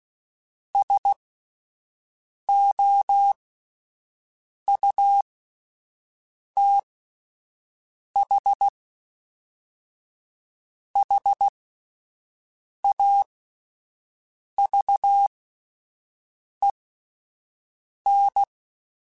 Morse code, heard as a keyed tone, about 16 wpm, spaced as for 5 wpm, 800 Hz. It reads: SOUTH HAVEN